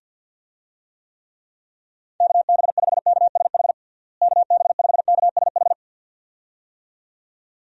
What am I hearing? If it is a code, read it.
Morse code, heard as a keyed tone, about 45 wpm, 700 Hz. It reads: KB5KSH KB5KSH
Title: USA Call Signs - 6 Characters - Repeated 2x - 50wpm